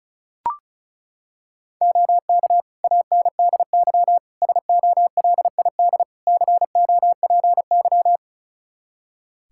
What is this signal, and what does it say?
Morse code, heard as a keyed tone, about 36 wpm, 700 Hz. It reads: OK ANDY SOLID COPY
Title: K1USN SST QSO Practice 35wpm